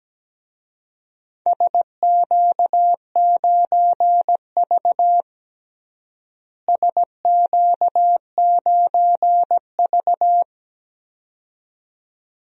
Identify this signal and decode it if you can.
Morse code, heard as a keyed tone, about 17 wpm, 700 Hz. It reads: SQ9V SQ9V
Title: DXCC-100 Call Signs - 4 Characters - Repeated 2x - 17wpm